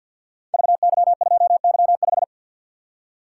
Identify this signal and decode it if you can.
Morse code, heard as a keyed tone, about 51 wpm, 700 Hz. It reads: VY2Y5
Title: Code Groups - 5 Letter:Numbers - 50wpm